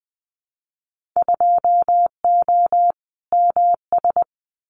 Morse code, heard as a keyed tone, about 19 wpm, 700 Hz. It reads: 2O MS
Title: Winter Field Day Exchanges 20wpm